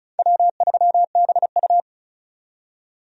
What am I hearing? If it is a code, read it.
Morse code, heard as a keyed tone, about 34 wpm, 700 Hz. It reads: W3BU
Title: USA Call Signs - 4 Characters - 35wpm